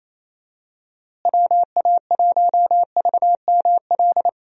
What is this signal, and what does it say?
Morse code, heard as a keyed tone, about 28 wpm, 700 Hz. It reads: WA1VML
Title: USA Call Signs - 6 Characters - Repeated 3x - 28wpm